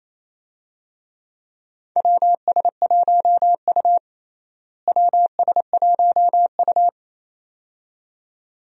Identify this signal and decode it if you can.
Morse code, heard as a keyed tone, about 28 wpm, 700 Hz. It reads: WS1U WS1U
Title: USA Call Signs - 4 Characters - Repeated 2x - 28wpm